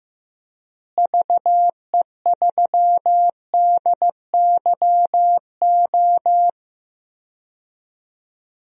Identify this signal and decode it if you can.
Morse code, heard as a keyed tone, about 15 wpm, 700 Hz. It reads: VE3DYO
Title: DXCC-100 Call Signs - 6 Characters - Repeated 3x - 15wpm